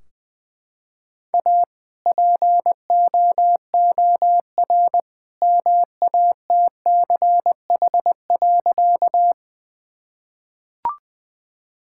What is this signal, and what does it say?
Morse code, heard as a keyed tone, about 20 wpm, 700 Hz. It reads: A POOR MATCH.
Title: Sentences from Top 1600 Words 20wpm